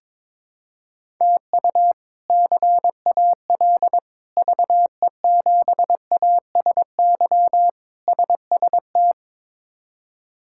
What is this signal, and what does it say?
Morse code, heard as a keyed tone, about 22 wpm, 700 Hz. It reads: TU CAL VE7ASY SST
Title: K1USN SST QSO Practice - 22wpm